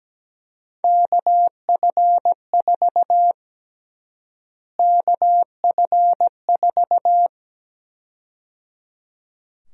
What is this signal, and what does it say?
Morse code, heard as a keyed tone, about 17 wpm, 700 Hz. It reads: KF4 KF4
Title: USA Call Sign Prefixes - Repeated 2x - 17wpm